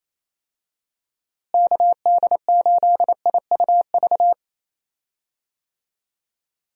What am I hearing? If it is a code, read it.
Morse code, heard as a keyed tone, about 27 wpm, 700 Hz. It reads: KD8IUV
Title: USA Call Signs - 6 Characters - Repeated 2x - 28wpm